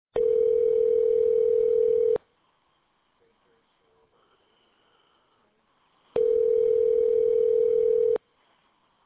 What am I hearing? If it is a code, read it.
Ringback tone (North America), 440 + 480 Hz.